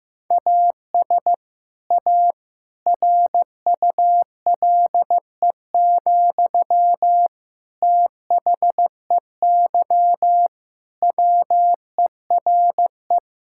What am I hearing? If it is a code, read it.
Morse code, heard as a keyed tone, about 16 wpm, 700 Hz. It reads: AS A RULE, THEY WERE
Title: Sentences from Top 1000 Words - Review Part 6 - 15wpm